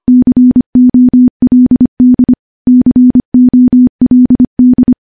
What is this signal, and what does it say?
Morse code, heard as a keyed tone, about 25 wpm, 250 Hz. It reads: COLD COLD
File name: morse-training-25WPM